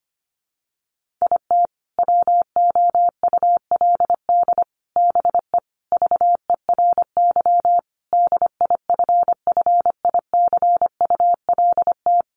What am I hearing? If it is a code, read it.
Morse code, heard as a keyed tone, about 25 wpm, 700 Hz. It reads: IT WOULD BE VERY DIFFICULT
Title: Sentences from Top 1100 Words 25wpm